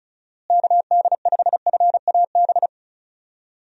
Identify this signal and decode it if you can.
Morse code, heard as a keyed tone, about 34 wpm, 700 Hz. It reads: KD5FAB
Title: USA Call Signs - 6 Characters - Repeated 2x - 35wpm